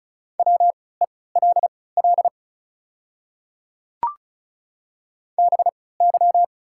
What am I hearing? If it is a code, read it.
Morse code, heard as a keyed tone, about 36 wpm, spaced as for 20 wpm, 700 Hz. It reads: WELL BY
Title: 3x Character Spacing - Top 200 Words - 35wpm